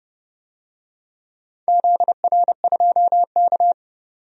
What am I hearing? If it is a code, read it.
Morse code, heard as a keyed tone, about 31 wpm, 700 Hz. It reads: ZR2K
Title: DXCC-100 Call Signs - 4 Characters - 30wpm